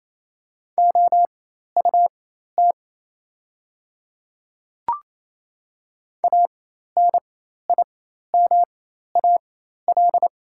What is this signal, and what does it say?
Morse code, heard as a keyed tone, about 28 wpm, spaced as for 13 wpm, 690 Hz. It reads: OUT ANIMAL
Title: 4x Character Spacing - Top 200 Words - 28wpm